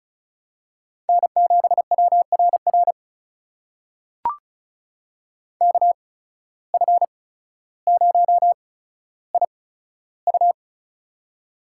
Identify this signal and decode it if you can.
Morse code, heard as a keyed tone, about 34 wpm, spaced as for 15 wpm, 700 Hz. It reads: N7WRR KF0IU
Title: USA Call Signs - 8x Character Spacing - 35wpm